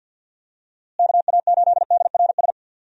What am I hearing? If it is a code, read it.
Morse code, heard as a keyed tone, about 51 wpm, 700 Hz. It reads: KA8DRS
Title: USA Call Signs - 6 Characters - Repeated 2x - 50wpm